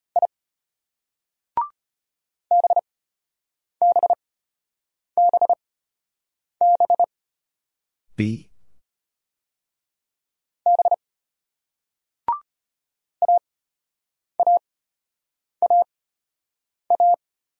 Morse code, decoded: I BBBB B AAAA

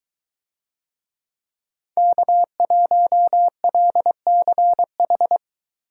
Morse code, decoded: K1LCH

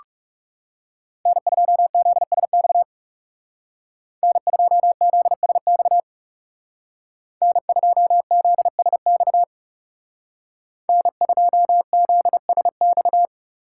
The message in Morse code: N2ZSX N2ZSX N2ZSX N2ZSX

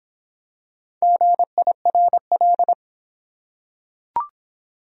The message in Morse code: GIRL

